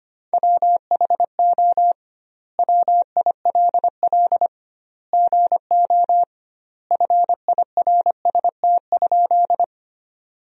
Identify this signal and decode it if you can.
Morse code, heard as a keyed tone, about 25 wpm, 700 Hz. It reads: WHO WILL GO FIRST?
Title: 2x Word-Spacing - Sentences from Top 100 Words - 25wpm